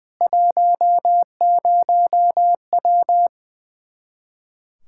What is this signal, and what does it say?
Morse code, heard as a keyed tone, about 21 wpm, 700 Hz. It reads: 10W
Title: K1USN SST QSO Practice 20wpm